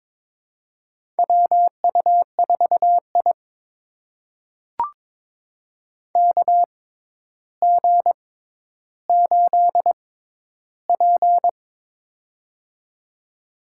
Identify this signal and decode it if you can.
Morse code, heard as a keyed tone, about 22 wpm, spaced as for 11 wpm, 700 Hz. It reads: WU4I KG8P
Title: USA Call Signs - 4 Characters - 6x Character Spacing - 22wpm